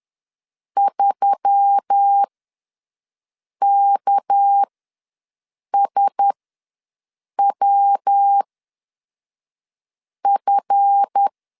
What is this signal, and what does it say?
Morse code, heard as a keyed tone, about 10 wpm, spaced as for 5 wpm, 800 Hz. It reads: 3KSWF
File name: LEZIONI / 7 GF9Z / 7 GF ws 10wpm 3918